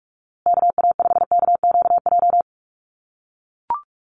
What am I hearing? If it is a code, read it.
Morse code, heard as a keyed tone, about 44 wpm, 700 Hz. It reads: KA5KQJ
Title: USA Call Signs Repeated 2x 45wpm